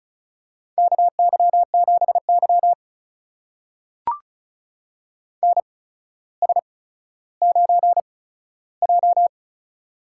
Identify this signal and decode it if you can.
Morse code, heard as a keyed tone, about 35 wpm, spaced as for 15 wpm, 700 Hz. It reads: KY7Y NS9J
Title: USA Call Signs - 4 Characters - 8x Character Spacing - 35wpm